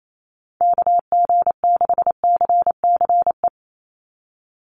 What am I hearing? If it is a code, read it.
Morse code, heard as a keyed tone, about 28 wpm, 700 Hz. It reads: KG6CCE
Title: USA Call Signs 28wpm